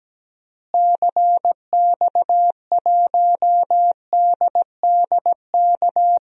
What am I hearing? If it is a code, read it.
Morse code, heard as a keyed tone, about 17 wpm, 700 Hz. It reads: CX1DDK
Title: DXCC-100 Call Signs - 6 Characters - Repeated 3x - 17wpm